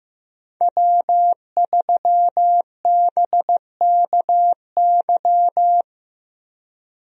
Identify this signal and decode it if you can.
Morse code, heard as a keyed tone, about 15 wpm, 700 Hz. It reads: W3BKY
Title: USA Call Signs - 5 Characters - Repeated 2x - 15wpm